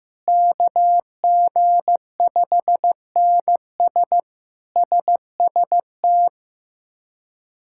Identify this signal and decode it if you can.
Morse code, heard as a keyed tone, about 15 wpm, 700 Hz. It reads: KG5NS SST
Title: K1USN SST QSO Practice - 15wpm